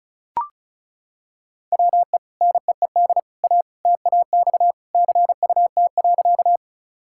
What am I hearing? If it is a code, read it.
Morse code, heard as a keyed tone, about 36 wpm, 700 Hz. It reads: WE NEED A TAX CUT.
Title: Sentences from Top 1000 Words - Review Part 5 - 35wpm